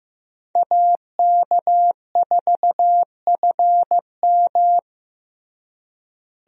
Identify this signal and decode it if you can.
Morse code, heard as a keyed tone, about 15 wpm, 700 Hz. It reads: AK4FM
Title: USA Call Signs - 5 Characters - Repeated 2x - 15wpm